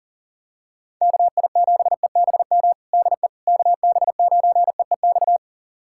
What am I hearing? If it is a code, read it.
Morse code, heard as a keyed tone, about 40 wpm, 700 Hz. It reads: KI7EBM DE KB9EEX
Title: USA Callsign DE Callsign - 6 Characters - 40wpm